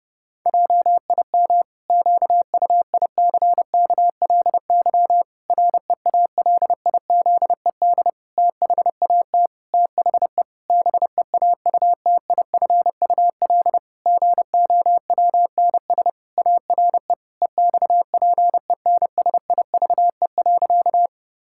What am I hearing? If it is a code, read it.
Morse code, heard as a keyed tone, about 30 wpm, 700 Hz. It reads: JIM QUICKLY REALIZED THAT THE BEAUTIFUL GOWNS ARE EXPENSIVE.
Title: Pangram Challenge v2 - 30wpm